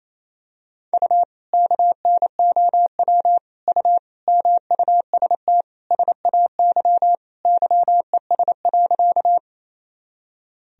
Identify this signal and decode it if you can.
Morse code, heard as a keyed tone, about 28 wpm, 700 Hz. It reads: U KNOW U MUST SAY YES.
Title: Sentences from Top 500 Words and Abbreviations 28wpm